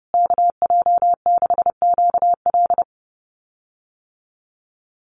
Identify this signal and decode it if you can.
Morse code, heard as a keyed tone, about 30 wpm, 700 Hz. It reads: KJ6QL
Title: USA Call Signs Repeated 3x 30wpm